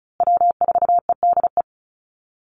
Morse code, heard as a keyed tone, about 34 wpm, 700 Hz. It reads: W4EDE
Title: Call Signs Repeated 2x - 35wpm